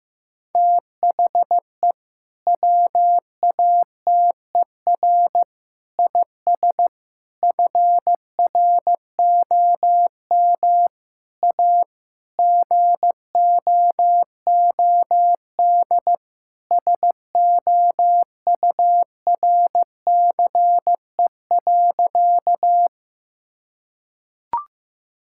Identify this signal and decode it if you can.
Morse code, heard as a keyed tone, about 15 wpm, 700 Hz. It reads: THE WATER IS FROM A GOOD SOURCE.
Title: Sentences from Top 1000 Words - Review Part 5 - 15wpm